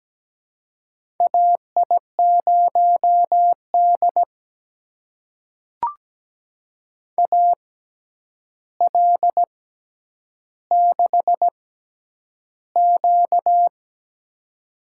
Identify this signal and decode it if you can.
Morse code, heard as a keyed tone, about 17 wpm, spaced as for 9 wpm, 700 Hz. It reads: AI0D AL6Q
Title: USA Call Signs - 4 Characters - 6x Character Spacing - 17wpm